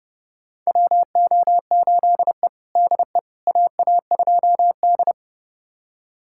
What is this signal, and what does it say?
Morse code, heard as a keyed tone, about 31 wpm, 700 Hz. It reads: WO8E DE AA2D
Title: USA Callsign DE Callsign - 4 Characters - 30wpm